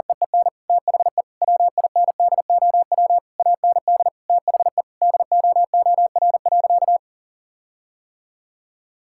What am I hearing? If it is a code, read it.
Morse code, heard as a keyed tone, about 40 wpm, 700 Hz. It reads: EEN THE WINDOW AND THE DOOR.
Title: Sentences from Top 1600 Words 40wpm